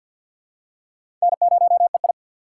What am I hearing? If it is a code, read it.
Morse code, heard as a keyed tone, about 49 wpm, 700 Hz. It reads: N0EI
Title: USA Call Signs - 4x Character Spacing - 50wpm